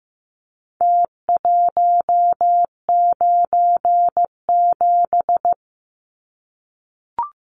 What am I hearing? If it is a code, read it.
Morse code, heard as a keyed tone, about 15 wpm, 700 Hz. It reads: T197
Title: Numbers - Contest Style - 4-Digit 15wpm